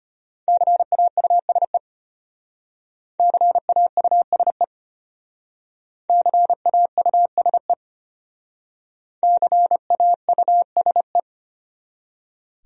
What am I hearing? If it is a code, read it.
Morse code, heard as a keyed tone, about 32 wpm, 700 Hz. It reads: CAUSE CAUSE CAUSE CAUSE